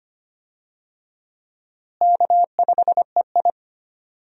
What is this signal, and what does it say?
Morse code, heard as a keyed tone, about 25 wpm, 700 Hz. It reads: K5EI
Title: USA Call Signs - 4 Characters - Repeated 3x - 25wpm